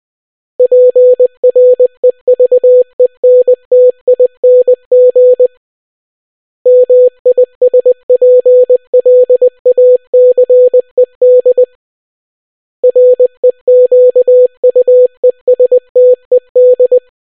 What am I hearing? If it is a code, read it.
Morse code, heard as a keyed tone, about 20 wpm, 500 Hz. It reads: PREVENTING MISPLACED REQUESTED